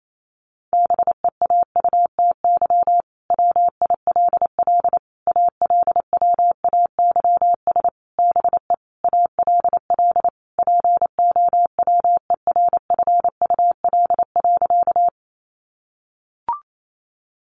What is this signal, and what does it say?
Morse code, heard as a keyed tone, about 28 wpm, 700 Hz. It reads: BEAUTY WILL ALWAYS BE ALL POWERFUL.
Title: Sentences from Top 1300 Words 28wpm